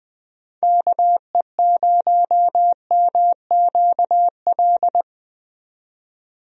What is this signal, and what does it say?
Morse code, heard as a keyed tone, about 20 wpm, 700 Hz. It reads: KE0MQL
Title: USA Call Signs - 6 Characters - 20wpm